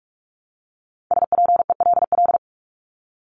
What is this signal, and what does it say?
Morse code, heard as a keyed tone, about 44 wpm, 700 Hz. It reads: SPELL
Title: Top 500 Words Repeated 45wpm